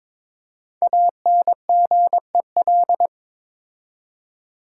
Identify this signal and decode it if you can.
Morse code, heard as a keyed tone, about 22 wpm, 700 Hz. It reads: ANGEL
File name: 5 Letter Words - First 3 Letters Given - 22wpm